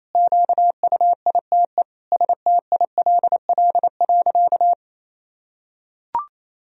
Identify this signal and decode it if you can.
Morse code, heard as a keyed tone, about 28 wpm, 700 Hz. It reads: QUITE STILL.
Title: Sentences from Top 1000 Words - Review Part 2 - 28wpm